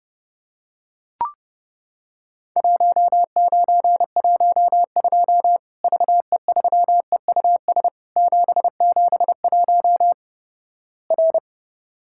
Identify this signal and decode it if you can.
Morse code, heard as a keyed tone, about 30 wpm, 700 Hz. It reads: 1912 VE3EUS 771 R
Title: Worked All Europe DX Contest - QTC Exchange - 30wpm